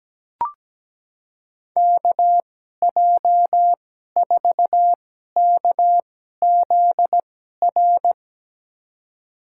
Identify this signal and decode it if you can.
Morse code, heard as a keyed tone, about 17 wpm, spaced as for 12 wpm, 700 Hz. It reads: KJ4KZR
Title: USA Call Signs - 6 Characters - 2x Character Spacing - 17wpm